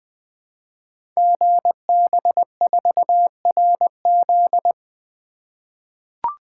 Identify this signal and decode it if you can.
Morse code, heard as a keyed tone, about 21 wpm, 700 Hz. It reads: GB4RZ
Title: Code Groups - 5 Letter:Numbers - 20wpm